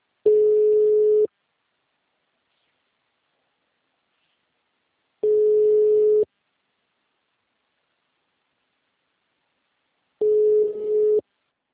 Ringback tone (Europe, 425 Hz).